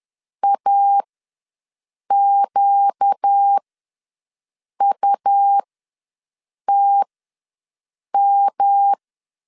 Morse code, heard as a keyed tone, about 10 wpm, spaced as for 6 wpm, 800 Hz. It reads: AQUTM
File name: LEZIONI / 6 PKX0 / 6 PKX0 ws 10wpm 4220